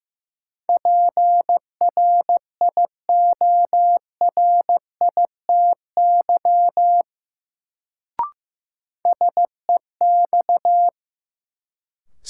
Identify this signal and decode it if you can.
Morse code, heard as a keyed tone, about 15 wpm, 700 Hz. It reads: PRIORITY SEX
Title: Sentences from Top 1600 Words - New Word Drill Alt - 15wpm